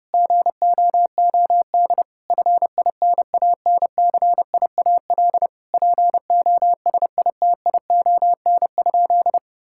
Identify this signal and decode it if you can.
Morse code, heard as a keyed tone, about 32 wpm, 700 Hz. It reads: GOOD FINANCIAL POSITION?
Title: Sentences from Top 1000 Words - Review Part 5 - 30wpm